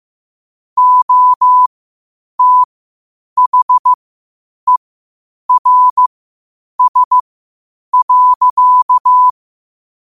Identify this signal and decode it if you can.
Morse code, heard as a keyed tone, about 15 wpm, spaced as for 8 wpm, 1000 Hz. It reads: OTHERS.
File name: qotdfiles / 8wpm